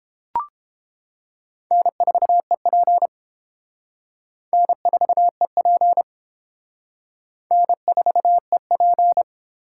Morse code, heard as a keyed tone, about 31 wpm, 700 Hz. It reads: N4EP N4EP N4EP